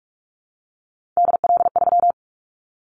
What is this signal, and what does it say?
Morse code, heard as a keyed tone, about 44 wpm, 700 Hz. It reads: DL3
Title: Callsign Prefixes for Top 25 DXCC - 45wpm